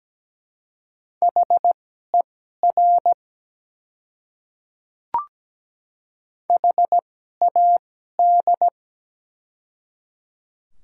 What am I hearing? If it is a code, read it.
Morse code, heard as a keyed tone, about 17 wpm, spaced as for 12 wpm, 700 Hz. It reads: HER HAD
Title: 2x Character Spacing - Top 100 Words - 17wpm